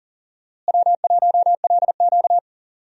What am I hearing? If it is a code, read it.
Morse code, heard as a keyed tone, about 40 wpm, 700 Hz. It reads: W1LQ
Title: USA Call Signs - 4 Characters - Repeated 3x - 40wpm